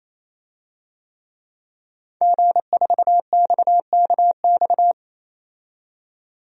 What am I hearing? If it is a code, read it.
Morse code, heard as a keyed tone, about 28 wpm, 700 Hz. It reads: G4XKX
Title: DXCC-100 Call Signs - 5 Characters - 28wpm